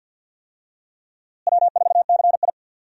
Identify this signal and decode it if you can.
Morse code, heard as a keyed tone, about 50 wpm, 700 Hz. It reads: W4CI
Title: USA Call Signs - 4 Characters - Repeated 2x - 50wpm